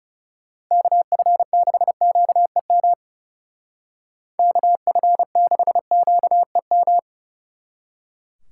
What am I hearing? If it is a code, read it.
Morse code, heard as a keyed tone, about 36 wpm, 700 Hz. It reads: KF6QEM KF6QEM
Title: Speed Racing - USA 2x3 Call Signs 30wpm